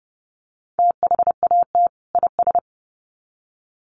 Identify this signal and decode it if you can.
Morse code, heard as a keyed tone, about 30 wpm, 700 Hz. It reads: THAT IS
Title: Sets of 2 Words 30wpm